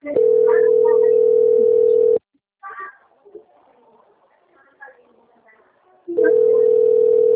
Ringback tone (North America), 440 + 480 Hz.
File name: out- - - - - .4546